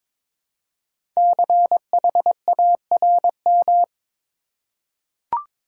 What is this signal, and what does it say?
Morse code, heard as a keyed tone, about 22 wpm, 700 Hz. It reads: CHARM
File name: 5 Letter Words - First 3 Letters Given - 22wpm